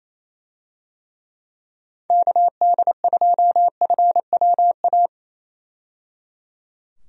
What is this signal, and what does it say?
Morse code, heard as a keyed tone, about 27 wpm, 700 Hz. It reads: KD2FWA
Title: USA Call Signs - 6 Characters - 28wpm